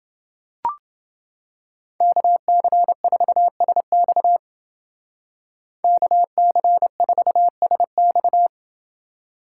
Morse code, heard as a keyed tone, about 29 wpm, 700 Hz. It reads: KC4SX KC4SX